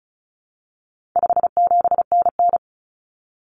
Morse code, heard as a keyed tone, about 34 wpm, 700 Hz. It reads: 57NN
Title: Common Cut Numbers - 4-Digit 35wpm